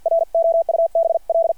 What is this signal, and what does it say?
Morse code, heard as a keyed tone, about 50 wpm, 660 Hz. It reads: ROUBL
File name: archived / CRSnet